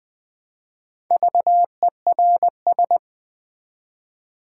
Morse code, heard as a keyed tone, about 20 wpm, 700 Hz. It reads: VERS
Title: Build A Word - 1 Letter At A Time - 5-Letter Words - 20wpm